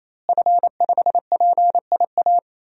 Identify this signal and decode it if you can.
Morse code, heard as a keyed tone, about 28 wpm, 700 Hz. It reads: F5PIA
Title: DXCC-100 Call Signs - 5 Characters - Repeated 3x - 28wpm